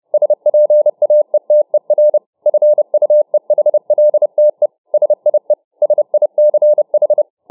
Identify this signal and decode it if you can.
Morse code, heard as a keyed tone, about 31 wpm, 600 Hz. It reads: SPAETER FUEHLTE SIE SICH